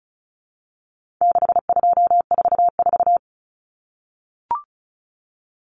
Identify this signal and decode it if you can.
Morse code, heard as a keyed tone, about 34 wpm, 700 Hz. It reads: 6244
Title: Numbers - 4-Digit 35wpm